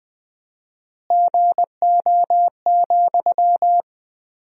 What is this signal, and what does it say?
Morse code, heard as a keyed tone, about 21 wpm, 700 Hz. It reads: GO,
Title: 4x Word-Spacing - Sentences from Top 400 Words - 20wpm